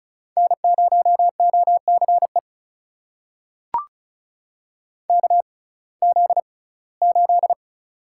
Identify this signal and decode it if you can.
Morse code, heard as a keyed tone, about 36 wpm, spaced as for 21 wpm, 700 Hz. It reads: N0OCE KZ8
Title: USA Call Signs - 6x Character Spacing - 35wpm